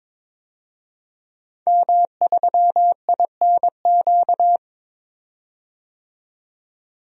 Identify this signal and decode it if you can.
Morse code, heard as a keyed tone, about 22 wpm, 700 Hz. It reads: M3INQ
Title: DXCC-100 Call Signs - 5 Characters - 22wpm